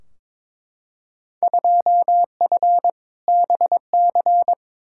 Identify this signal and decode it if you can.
Morse code, heard as a keyed tone, about 22 wpm, 700 Hz. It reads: 2F BC
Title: Field Day Exchanges Encore - 22wpm